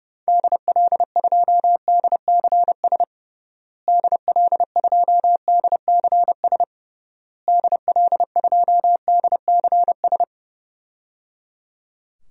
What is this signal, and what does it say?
Morse code, heard as a keyed tone, about 32 wpm, 700 Hz. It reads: DL2DCS DL2DCS DL2DCS
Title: DXCC-100 Call Signs - 6 Characters - Repeated 3x - 30wpm